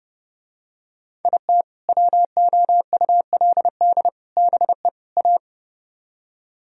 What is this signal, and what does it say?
Morse code, heard as a keyed tone, about 29 wpm, 700 Hz. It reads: IT WOULD BE A